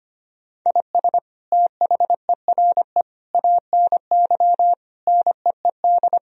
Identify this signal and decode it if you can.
Morse code, heard as a keyed tone, about 25 wpm, 700 Hz. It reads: IS THERE ANY NEED
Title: Sentences from Top 1000 Words - Review Part 4 - 25wpm